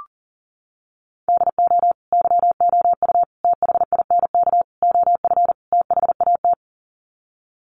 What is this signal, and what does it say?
Morse code, heard as a keyed tone, about 40 wpm, 700 Hz. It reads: DO YOU THINK OF THAT
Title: Sets of 5 Words Encore2 40wpm